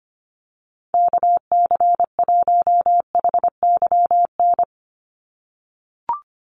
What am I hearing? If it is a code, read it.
Morse code, heard as a keyed tone, about 25 wpm, 700 Hz. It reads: KC1HYN
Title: USA Call Signs 25wpm